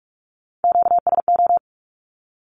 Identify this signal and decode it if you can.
Morse code, heard as a keyed tone, about 46 wpm, 700 Hz. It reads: QSO
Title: Q-Codes with Spoken Meaning - 45wpm